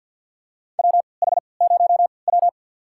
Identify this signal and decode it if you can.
Morse code, heard as a keyed tone, about 46 wpm, spaced as for 27 wpm, 700 Hz. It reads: WH0W
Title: USA Call Signs - 4 Characters - 3x Character Spacing - 50wpm